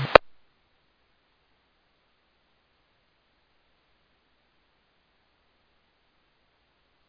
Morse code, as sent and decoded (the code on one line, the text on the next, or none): none